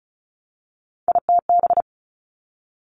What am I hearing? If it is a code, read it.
Morse code, heard as a keyed tone, about 36 wpm, 700 Hz. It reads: ITB
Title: Three Challenging Letter-Number Combinations 35wpm